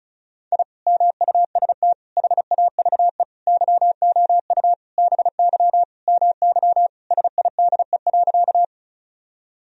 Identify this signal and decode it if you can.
Morse code, heard as a keyed tone, about 36 wpm, 700 Hz. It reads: I MUST HAVE YOU BY MY SIDE.
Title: Sentences from Top 1000 Words - Review Part 1 - 35wpm